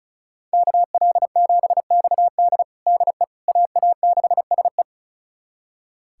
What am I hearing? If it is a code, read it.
Morse code, heard as a keyed tone, about 36 wpm, 700 Hz. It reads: KL7XD DE AA6SE
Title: USA Callsign DE Callsign - 5 Characters - 35wpm